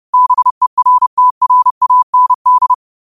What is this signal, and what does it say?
Morse code, heard as a keyed tone, about 30 wpm, 1000 Hz. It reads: BERTRAND
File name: 30wpm